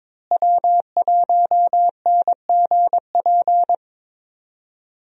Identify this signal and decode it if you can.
Morse code, heard as a keyed tone, about 22 wpm, 700 Hz. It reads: W1NGP
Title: USA Call Signs - 5 Characters - Repeated 3x - 22wpm